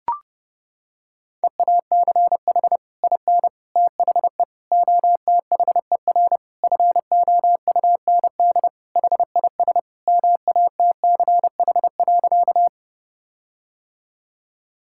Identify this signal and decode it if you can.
Morse code, heard as a keyed tone, about 30 wpm, 700 Hz. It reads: EACH IN THE OTHER FOUND HIS MATCH.
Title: Sentences from Top 1600 Words 30wpm